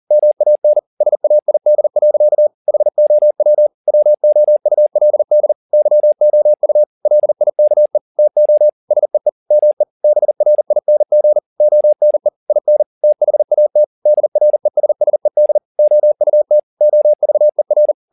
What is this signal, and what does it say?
Morse code, heard as a keyed tone, about 40 wpm, 600 Hz. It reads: MAN SAID. HOW WOULD YOU LIKE TO SEE ME BRING ONE IN THAT DRESSED OUT OVER